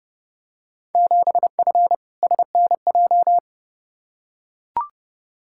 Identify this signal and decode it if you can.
Morse code, heard as a keyed tone, about 32 wpm, 700 Hz. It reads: 7F SNJ
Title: Field Day Exchanges Encore - 30wpm